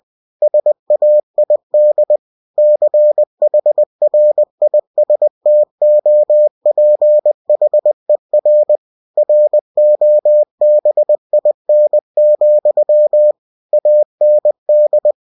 Morse code, heard as a keyed tone, about 20 wpm, 600 Hz. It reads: SAID CHRISTOPHER ROBIN, AND
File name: cw-books / winnie-the-pooh / winnie-20wpm-page-0088